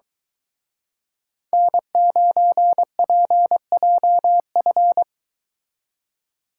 Morse code, heard as a keyed tone, about 26 wpm, 700 Hz. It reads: N9PJF